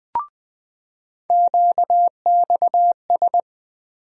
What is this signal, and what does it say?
Morse code, heard as a keyed tone, about 20 wpm, 700 Hz. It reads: QXS
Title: USA Call Sign Postfixes - 3 Letters 20wpm